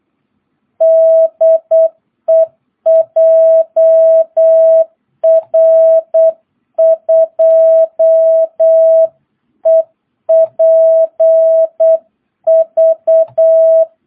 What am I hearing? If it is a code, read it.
Morse code, heard as a keyed tone, about 8 wpm, 660 Hz. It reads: DE JR2EPV
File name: jh8cbh / 2022cw